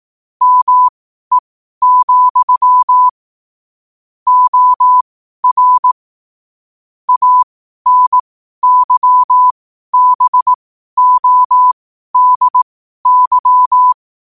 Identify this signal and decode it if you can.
Morse code, heard as a keyed tone, about 18 wpm, spaced as for 12 wpm, 1000 Hz. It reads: ME, OR ANYBODY
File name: qotdfiles / 12wpm